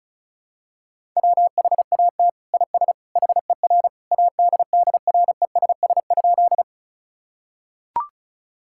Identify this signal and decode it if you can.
Morse code, heard as a keyed tone, about 34 wpm, 700 Hz. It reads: WHAT IS HER ADDRESS?
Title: Sentences from Top 1000 Words - Review Part 6 - 35wpm